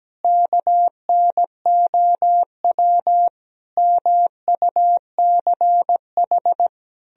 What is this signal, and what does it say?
Morse code, heard as a keyed tone, about 17 wpm, 700 Hz. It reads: KNOW MUCH
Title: Sentences from Top 1000 Words - Review Part 2 - 17wpm